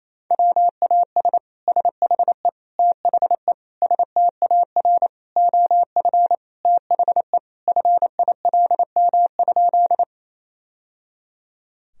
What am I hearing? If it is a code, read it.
Morse code, heard as a keyed tone, about 27 wpm, 700 Hz. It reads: WAS SHE THE STAR OF THE FILM?
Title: Sentences from Top 1000 Words - Review Part 3 - 28wpm